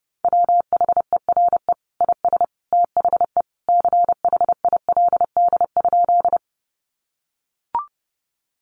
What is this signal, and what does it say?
Morse code, heard as a keyed tone, about 30 wpm, 700 Hz. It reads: WHERE IS THE CHILD?
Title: Sentences from Top 800 Words 30wpm